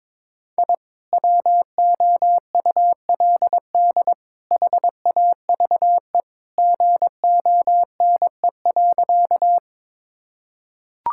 Morse code, heard as a keyed tone, about 22 wpm, 700 Hz. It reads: I WOULD HAVE GONE.
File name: practice / 22wpm / Sentences with 3-Word Phrases - 22wpm